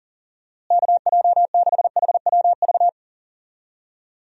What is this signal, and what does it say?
Morse code, heard as a keyed tone, about 40 wpm, 700 Hz. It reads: KJ6HWV
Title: USA Call Signs - 6 Characters - Repeated 2x - 40wpm